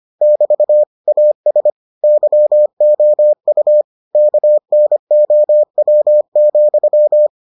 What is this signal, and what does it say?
Morse code, heard as a keyed tone, about 25 wpm, 600 Hz. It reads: <BT> AS YOU KNOW,